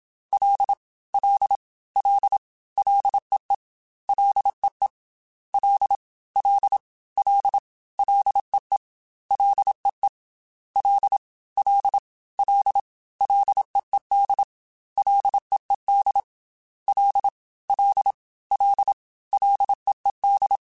Morse code, heard as a keyed tone, about 27 wpm, 800 Hz. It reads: L L L LEE LEE L L L LEE LEE L L L LEED LEED L L L LEED
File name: Ls / Single L m3s / Single L @25